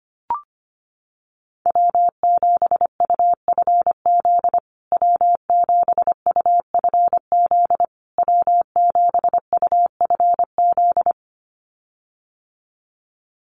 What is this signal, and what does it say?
Morse code, heard as a keyed tone, about 25 wpm, 700 Hz. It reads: W7UFZ W7UFZ W7UFZ
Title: USA Call Signs Repeated 3x 25wpm